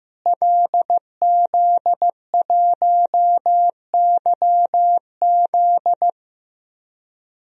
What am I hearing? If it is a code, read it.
Morse code, heard as a keyed tone, about 15 wpm, 700 Hz. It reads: LZ1YZ
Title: DXCC-100 Call Signs - 5 Characters - Repeated 2x - 15wpm